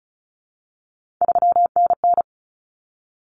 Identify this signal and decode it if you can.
Morse code, heard as a keyed tone, about 34 wpm, 700 Hz. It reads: 3NN
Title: Common Cut Numbers - 3-Digit 35wpm